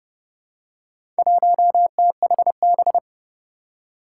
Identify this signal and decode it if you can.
Morse code, heard as a keyed tone, about 32 wpm, 700 Hz. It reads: 1THB
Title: Four Challenging Letter-Number Combinations 30wpm